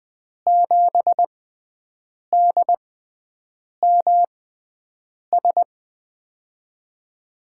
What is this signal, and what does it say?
Morse code, heard as a keyed tone, about 20 wpm, spaced as for 7 wpm, 700 Hz. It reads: 7DMS
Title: USA Call Signs - 6 Characters - 6x Character Spacing - 20wpm